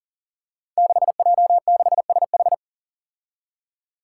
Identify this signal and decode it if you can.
Morse code, heard as a keyed tone, about 40 wpm, 700 Hz. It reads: 6J6SH
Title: Five Challenging Letter-Number Combinations 40wpm